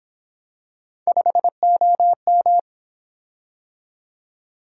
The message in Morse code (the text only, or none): 5OM